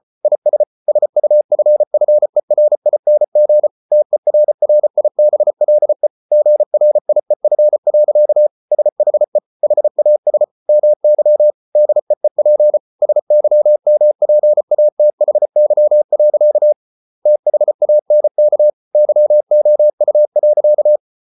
Morse code, heard as a keyed tone, about 34 wpm, 600 Hz. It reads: IS SUFFERING TERRIBLE GRIEF. SHE HAS MY DEEP SYMPATHY. THANK YOU.